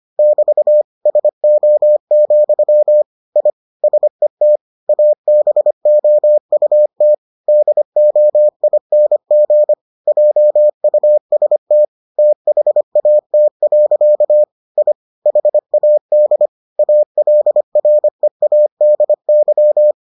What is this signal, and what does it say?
Morse code, heard as a keyed tone, about 25 wpm, 600 Hz. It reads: <BT> SO, I SET ABOUT DOING JUST THAT. I HAD ALREADY